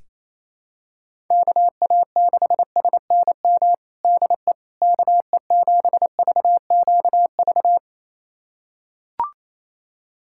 Morse code, heard as a keyed tone, about 28 wpm, 700 Hz. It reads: KA6SNM DE KE7VQV
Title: USA Callsign DE Callsign - 6 Characters - 28wpm